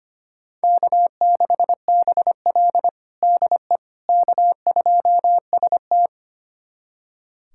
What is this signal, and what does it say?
Morse code, heard as a keyed tone, about 25 wpm, 700 Hz. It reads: K6BL DE K2ST
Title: USA Callsign DE Callsign - 4 Characters - 25wpm